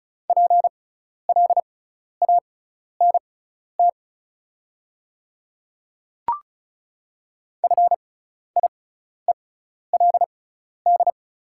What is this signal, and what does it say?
Morse code, heard as a keyed tone, about 34 wpm, spaced as for 12 wpm, 690 Hz. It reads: PLANT FIELD
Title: 6x Character Spacing - Top 500 Words - 35wpm